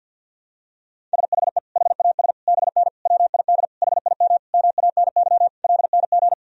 Morse code, heard as a keyed tone, about 50 wpm, 700 Hz. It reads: SHE HAS BN WID HIM MANY LNG
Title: Sentences from Top 200 Words and Abbreviations 50wpm